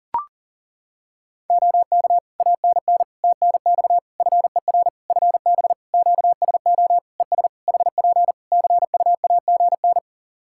Morse code, heard as a keyed tone, about 40 wpm, 700 Hz. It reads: OK ANN TNX FER FB QSO ES HP CUAGN
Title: Practice QSO - Exchange 4 - Ending - 40wpm